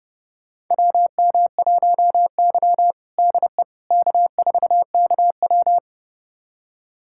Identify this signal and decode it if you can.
Morse code, heard as a keyed tone, about 30 wpm, 700 Hz. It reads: WM1Y DE K4KW
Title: USA Callsign DE Callsign - 4 Characters - 30wpm